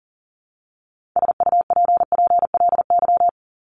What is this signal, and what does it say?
Morse code, heard as a keyed tone, about 40 wpm, 700 Hz. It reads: SUPPLY
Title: Top 501-750 Words 40wpm